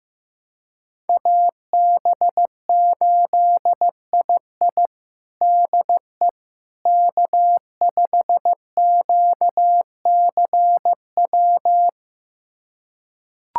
Morse code, heard as a keyed tone, about 15 wpm, 700 Hz. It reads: AB8II DE K5QCW
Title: USA Callsign DE Callsign - 5 Characters - 15wpm